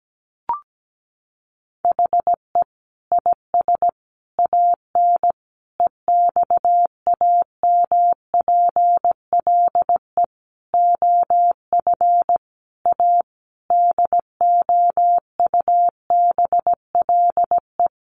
Morse code, heard as a keyed tone, about 17 wpm, 700 Hz. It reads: HE IS AN EXAMPLE OF A DOUBLE
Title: Sentences from Top 1500 Words 17wpm